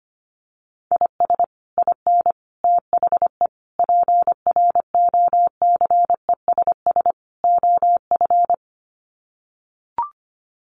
Morse code, heard as a keyed tone, about 25 wpm, 700 Hz. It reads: IS IN THE PROCESS OF
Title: Sets of 5 Words Encore2 25wpm